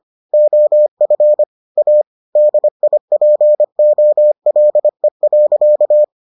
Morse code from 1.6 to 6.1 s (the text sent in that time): A DIPOLE.